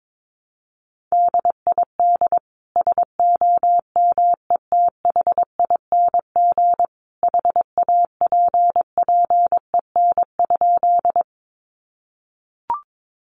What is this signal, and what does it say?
Morse code, heard as a keyed tone, about 22 wpm, 700 Hz. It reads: DID SOMETHING HAPPEN?
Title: Sentences from Top 600 Words 22wpm